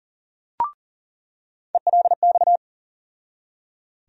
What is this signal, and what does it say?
Morse code, heard as a keyed tone, about 40 wpm, 700 Hz. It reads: ELX
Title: USA Call Sign Postfixes - 3 Letters 40wpm